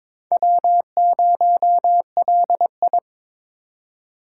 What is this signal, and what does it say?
Morse code, heard as a keyed tone, about 22 wpm, 700 Hz. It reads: W0LI
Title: USA Call Signs - 4 Characters - 4x Character Spacing - 22wpm